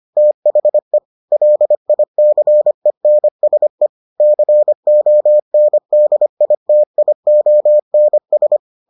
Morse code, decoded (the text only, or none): THE LICENSE CONDITIONS